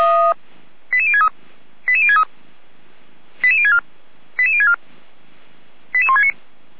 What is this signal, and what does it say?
Five-tone selective call (ZVEI), heard as quick runs of five tones, tones of about 70 ms. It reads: 88063 88063 88064 88064 88169
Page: SDR/ZVEI